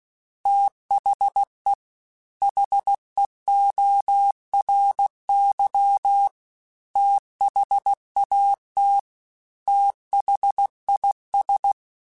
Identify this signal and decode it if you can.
Morse code, heard as a keyed tone, about 16 wpm, 800 Hz. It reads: THE HEORY THAT THIS
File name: TH Runs / TH Runs mp3s / TH Runs @15